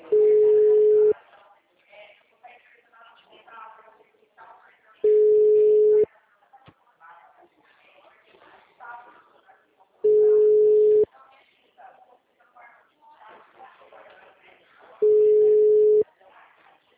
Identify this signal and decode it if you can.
Ringback tone (Europe, 425 Hz), with speech.